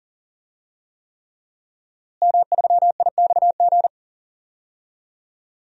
Morse code, heard as a keyed tone, about 40 wpm, 700 Hz. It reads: M3IXG
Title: DXCC-100 Call Signs - 5 Characters - Repeated 3x - 40wpm